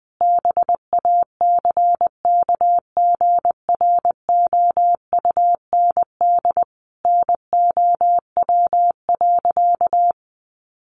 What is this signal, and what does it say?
Morse code, heard as a keyed tone, about 19 wpm, 700 Hz. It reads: BACKGROUND NOW.
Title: Sentences from Top 1300 Words 20wpm